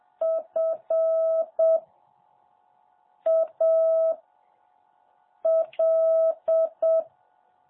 Morse code, heard as a keyed tone, about 7 wpm, 650 Hz. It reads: F A L
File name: jh8cbh / 2021cw